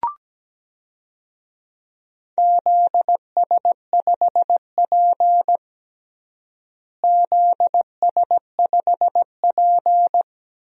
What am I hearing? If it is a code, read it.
Morse code, heard as a keyed tone, about 17 wpm, 700 Hz. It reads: ZS5P ZS5P
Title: DXCC-100 Call Signs - 4 Characters - Repeated 2x - 17wpm